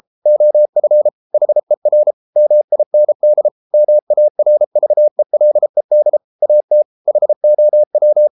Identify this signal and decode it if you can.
Morse code, heard as a keyed tone, about 33 wpm, 600 Hz. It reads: OF HER MIND MARVELED AT HOW